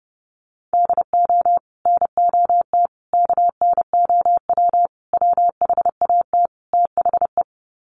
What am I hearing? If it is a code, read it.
Morse code, heard as a keyed tone, about 30 wpm, 700 Hz. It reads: DO NOT KNOW WHAT THE
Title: Sets of 5 Words 30wpm